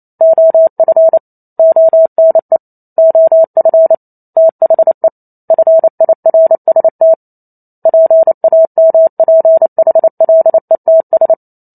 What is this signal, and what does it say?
Morse code, heard as a keyed tone, about 28 wpm, 650 Hz. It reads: OF ONE OF THE FIRST PAMPHLETS